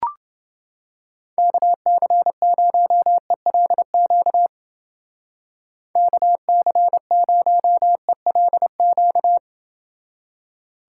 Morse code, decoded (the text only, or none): KC0ELQ KC0ELQ